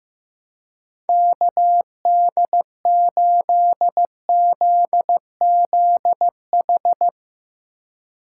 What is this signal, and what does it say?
Morse code, heard as a keyed tone, about 15 wpm, 700 Hz. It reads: KD8ZZH
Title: USA Call Signs - 6 Characters - 3x Character Spacing - 15wpm